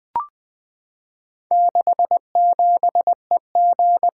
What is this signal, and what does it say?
Morse code, heard as a keyed tone, about 20 wpm, 700 Hz. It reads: 67EG
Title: Four Letter-Number Combinations - 20wpm